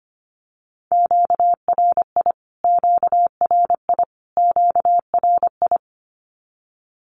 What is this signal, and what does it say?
Morse code, heard as a keyed tone, about 25 wpm, 700 Hz. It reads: QRI QRI QRI
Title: Q-Codes Repeated 3x 25wpm